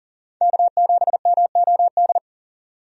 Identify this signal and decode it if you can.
Morse code, heard as a keyed tone, about 40 wpm, 700 Hz. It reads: K7MOD
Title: USA Call Signs - 5 Characters - 40wpm